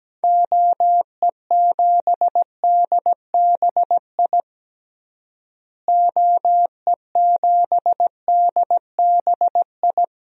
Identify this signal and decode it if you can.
Morse code, heard as a keyed tone, about 17 wpm, 700 Hz. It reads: OE7DBI OE7DBI
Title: DXCC-100 Call Signs - 6 Characters - Repeated 3x - 17wpm